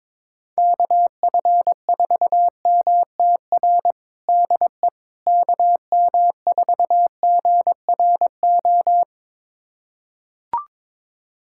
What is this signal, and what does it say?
Morse code, heard as a keyed tone, about 21 wpm, 700 Hz. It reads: KF4MTR DE KM4GRO
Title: USA Callsign DE Callsign 22wpm